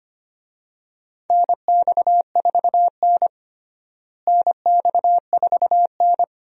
Morse code, heard as a keyed tone, about 25 wpm, 700 Hz. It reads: NX4N NX4N
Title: USA Call Signs - 4 Characters - Repeated 3x - 25wpm